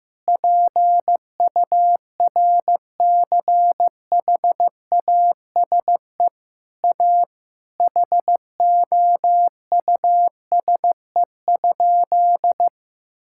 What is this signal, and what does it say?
Morse code, heard as a keyed tone, about 16 wpm, 700 Hz. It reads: PURCHASE A HOUSE?
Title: Sentences from Top 1000 Words - Review Part 6 - 15wpm